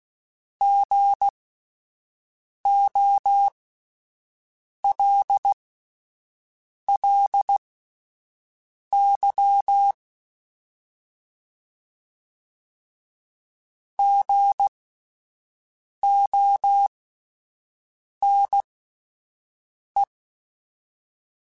Morse code, heard as a keyed tone, about 16 wpm, spaced as for 5 wpm, 800 Hz. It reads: GOLLY GONE
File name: Alphabetical / Alphabetical mp3s / Gs @5